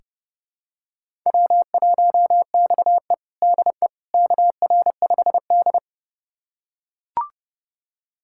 Morse code, heard as a keyed tone, about 32 wpm, 700 Hz. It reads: W1XE DE KR5D
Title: USA Callsign DE Callsign - 4 Characters - 30wpm